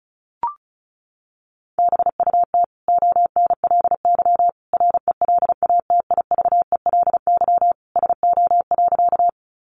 Morse code, heard as a keyed tone, about 34 wpm, 700 Hz. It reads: BUT ONLY RELATIVELY SO.
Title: Sentences from Top 1100 Words 35wpm